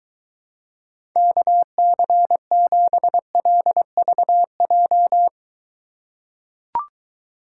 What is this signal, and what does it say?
Morse code, heard as a keyed tone, about 23 wpm, 700 Hz. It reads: KC7LVJ